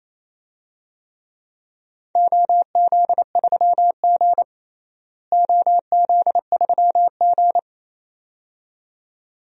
Morse code, heard as a keyed tone, about 28 wpm, 700 Hz. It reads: OZ3G OZ3G
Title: DXCC-100 Call Signs - 4 Characters - Repeated 2x - 28wpm